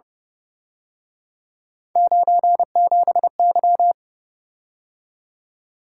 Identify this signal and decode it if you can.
Morse code, heard as a keyed tone, about 29 wpm, 700 Hz. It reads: 97Y